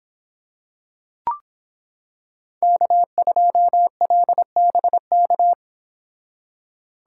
Morse code, heard as a keyed tone, about 26 wpm, 700 Hz. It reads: K2LBK